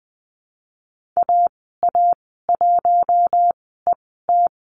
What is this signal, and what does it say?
Morse code, heard as a keyed tone, about 20 wpm, spaced as for 14 wpm, 700 Hz. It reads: AA1ET
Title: 062 Lesson 11g - Letter h - Call Signs 20wpm